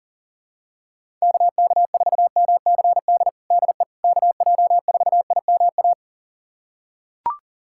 Morse code, heard as a keyed tone, about 40 wpm, 700 Hz. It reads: KK4MCD DE KJ4IMA
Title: USA Callsign DE Callsign - 6 Characters - 40wpm